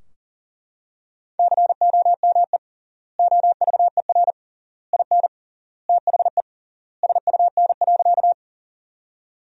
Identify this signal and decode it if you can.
Morse code, heard as a keyed tone, about 40 wpm, 700 Hz. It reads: COME OVER IN THE SUN.
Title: 3x Word-Spacing - Sentences from Top 300 Words - 40wpm